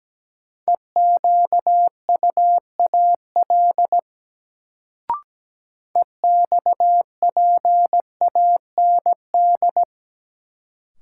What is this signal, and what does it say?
Morse code, heard as a keyed tone, about 17 wpm, 700 Hz. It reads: EQUAL EXPAND
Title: Sentences from Top 1400 Words - New Word Drill Alt - 17wpm